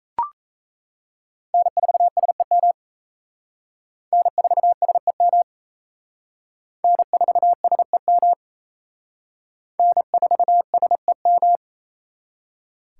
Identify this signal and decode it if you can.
Morse code, heard as a keyed tone, about 35 wpm, 700 Hz. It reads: N4SEM N4SEM N4SEM N4SEM
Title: Speed Racing - USA 1x3 Call Signs 28wpm